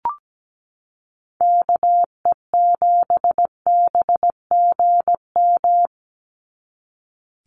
Morse code, heard as a keyed tone, about 17 wpm, 700 Hz. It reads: KE7BGM
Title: NA SOTA Activator Callsigns 17wpm